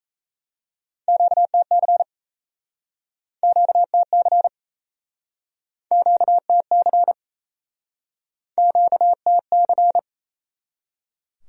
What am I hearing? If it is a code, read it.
Morse code, heard as a keyed tone, about 37 wpm, 700 Hz. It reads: QTC QTC QTC QTC